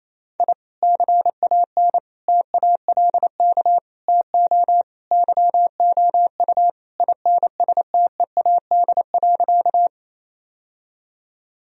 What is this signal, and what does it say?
Morse code, heard as a keyed tone, about 28 wpm, 700 Hz. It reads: I CAN TALK TO YOU INSTEAD.
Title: Sentences from Top 1000 Words - Review Part 4 - 28wpm